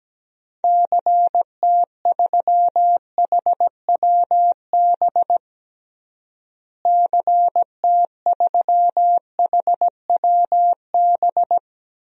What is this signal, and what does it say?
Morse code, heard as a keyed tone, about 17 wpm, 700 Hz. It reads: CT3HWB CT3HWB
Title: DXCC-100 Call Signs - 6 Characters - Repeated 3x - 17wpm